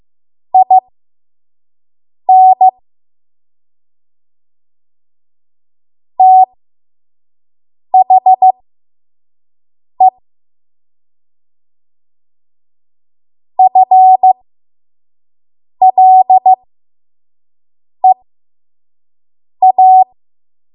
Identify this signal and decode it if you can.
Morse code, heard as a keyed tone, about 15 wpm, spaced as for 5 wpm, 750 Hz. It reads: IN THE FLEA
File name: Morse / 05WPM